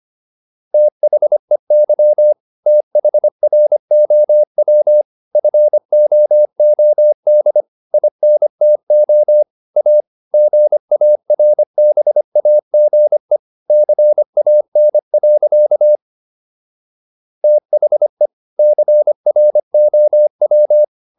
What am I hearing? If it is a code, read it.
Morse code, heard as a keyed tone, about 25 wpm, 600 Hz. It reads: THEY THROW FOOD INTO A GARBAGE CAN. THE CROW